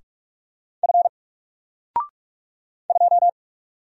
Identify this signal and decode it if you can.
Morse code, heard as a keyed tone, about 44 wpm, 700 Hz. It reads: F 2
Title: Single Letter-Number 45wpm